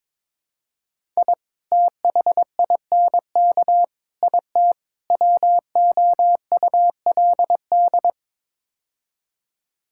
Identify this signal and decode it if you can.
Morse code, heard as a keyed tone, about 22 wpm, 700 Hz. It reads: I THINK IT WOULD
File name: practice / 22wpm / Build A Phrase - 5-Word Phrases - 22wpm